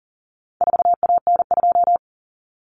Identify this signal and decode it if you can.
Morse code, heard as a keyed tone, about 40 wpm, 700 Hz. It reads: 4AN2
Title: Common Cut Numbers - 4-Digit 40wpm